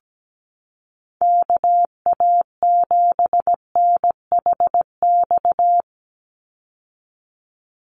Morse code, heard as a keyed tone, about 17 wpm, 700 Hz. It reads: KA7NHX
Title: USA Call Signs 17wpm